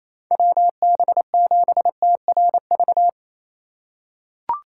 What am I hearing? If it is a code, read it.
Morse code, heard as a keyed tone, about 28 wpm, 700 Hz. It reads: WB7TRV
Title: USA Call Signs - 6 Characters - 3x Character Spacing - 28wpm